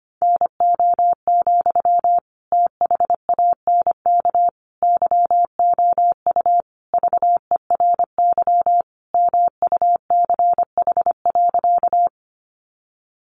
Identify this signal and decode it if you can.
Morse code, heard as a keyed tone, about 25 wpm, 700 Hz. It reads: NO, THANK YOU VERY MUCH.
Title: Sentences from Top 600 Words 25wpm